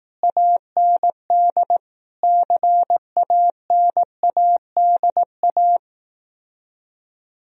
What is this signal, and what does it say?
Morse code, heard as a keyed tone, about 18 wpm, 700 Hz. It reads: AND CANADA
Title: Speed-Racing - Sets of 5 Words 15wpm